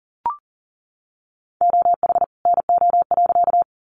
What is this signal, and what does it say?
Morse code, heard as a keyed tone, about 40 wpm, 700 Hz. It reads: OH NO.
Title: Additional Easy Phrases 40wpm